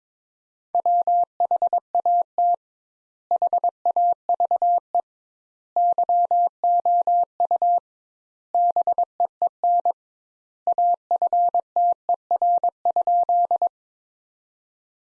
Morse code, heard as a keyed tone, about 22 wpm, 700 Hz. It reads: WHAT HAVE YOU BEEN AFTER?
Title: Sentences Top 200 Words - 22wpm